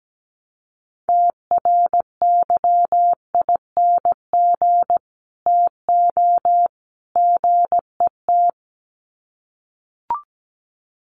Morse code, heard as a keyed tone, about 17 wpm, 700 Hz. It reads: TRYING TO GET
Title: Sets of 3 Words 17wpm